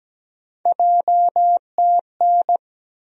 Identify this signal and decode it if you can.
Morse code, heard as a keyed tone, about 17 wpm, 700 Hz. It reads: JTN
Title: USA Call Sign Postfixes - 3 Letters 17wpm